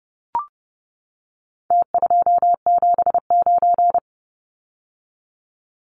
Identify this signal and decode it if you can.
Morse code, heard as a keyed tone, about 30 wpm, 700 Hz. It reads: T279
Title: Numbers - Contest Style - 4-Digit 30wpm